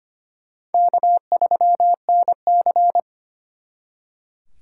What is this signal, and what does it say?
Morse code, heard as a keyed tone, about 31 wpm, 700 Hz. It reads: K3NC
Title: Speed Racing - USA 1x2 Call Signs 25wpm